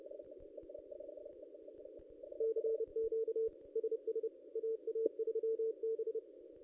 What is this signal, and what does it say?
Morse code, heard as a keyed tone, about 30 wpm, 450 Hz. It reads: CQ SS AA3B
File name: ss-cw-k5tr-op-n5rz